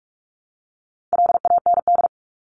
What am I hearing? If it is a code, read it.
Morse code, heard as a keyed tone, about 44 wpm, 700 Hz. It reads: LAND
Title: Top 500 Words 45wpm